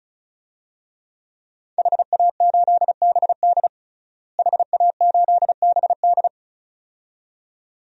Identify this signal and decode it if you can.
Morse code, heard as a keyed tone, about 36 wpm, 700 Hz. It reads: HA8BD HA8BD
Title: DXCC-100 Call Signs - 5 Characters - Repeated 2x - 35wpm